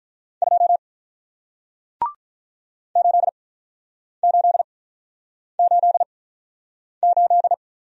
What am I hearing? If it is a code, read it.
Morse code, heard as a keyed tone, about 48 wpm, spaced as for 9 wpm, 700 Hz. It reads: 2 8888